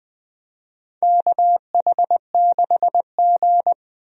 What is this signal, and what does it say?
Morse code, heard as a keyed tone, about 20 wpm, 700 Hz. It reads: KH6G
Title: DXCC-100 Call Signs - 4 Characters - 20wpm